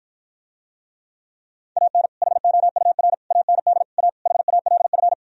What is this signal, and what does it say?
Morse code, heard as a keyed tone, about 52 wpm, 700 Hz. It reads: AN HOUR AND A HALF